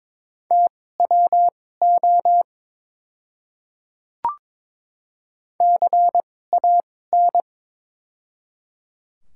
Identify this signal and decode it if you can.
Morse code, heard as a keyed tone, about 23 wpm, spaced as for 16 wpm, 700 Hz. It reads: TWO CAN
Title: 2x Character Spacing - Top 200 Words - 22wpm